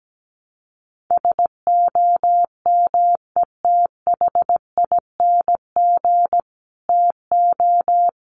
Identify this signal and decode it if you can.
Morse code, heard as a keyed tone, about 17 wpm, 700 Hz. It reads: SOMETHING TO
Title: Sets of 5 Words Encore 17wpm